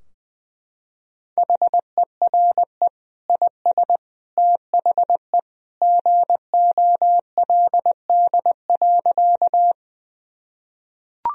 Morse code, heard as a keyed tone, about 20 wpm, 700 Hz. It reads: HERE IS THE GOLD.
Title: Sentences from Top 1000 Words - Review Part 3 - 20wpm